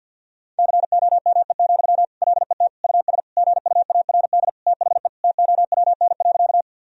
Morse code, heard as a keyed tone, about 51 wpm, 700 Hz. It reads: COME, LET US GUARD THE TOWN.
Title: Sentences from Top 1000 Words - Review Part 4 - 50wpm